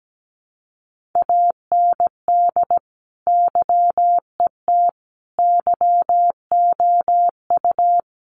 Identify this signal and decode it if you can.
Morse code, heard as a keyed tone, about 17 wpm, 700 Hz. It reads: AND YET YOU
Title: Sentences from Top 1400 Words 17wpm